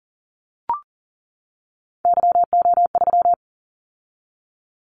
Morse code, heard as a keyed tone, about 39 wpm, 700 Hz. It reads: YO3
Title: Callsign Prefixes for Top 25 DXCC - 40wpm